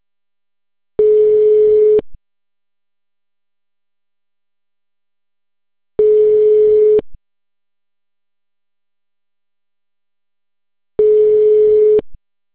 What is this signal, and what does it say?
Ringback tone (Europe, 425 Hz).